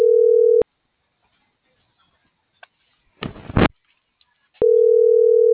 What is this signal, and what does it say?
Ringback tone (North America), 440 + 480 Hz.